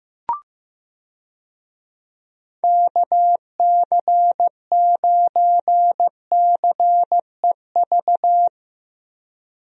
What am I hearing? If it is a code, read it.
Morse code, heard as a keyed tone, about 15 wpm, 700 Hz. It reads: KC9CEV
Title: USA Call Signs - 6 Characters - Repeated 2x - 15wpm